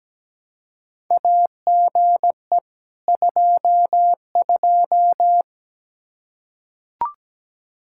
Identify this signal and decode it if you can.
Morse code, heard as a keyed tone, about 17 wpm, 700 Hz. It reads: AGE 22
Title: Practice QSO - Exchange 3 - More Info - 17wpm